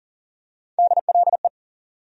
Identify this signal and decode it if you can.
Morse code, heard as a keyed tone, about 40 wpm, 700 Hz. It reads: DLE
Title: USA Call Sign Postfixes - 3 Letters 40wpm